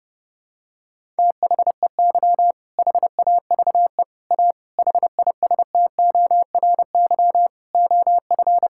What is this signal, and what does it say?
Morse code, heard as a keyed tone, about 30 wpm, 700 Hz. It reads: THEY HAVE A HISTORY OF
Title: Sentences from Top 1000 Words - Review Part 5 - 30wpm